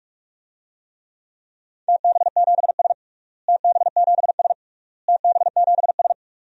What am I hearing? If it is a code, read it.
Morse code, heard as a keyed tone, about 46 wpm, 700 Hz. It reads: TB7S TB7S TB7S
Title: DXCC-100 Call Signs - 4 Characters - Repeated 3x - 45wpm